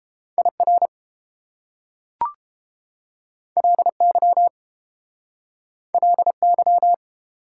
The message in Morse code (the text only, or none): IR LY LY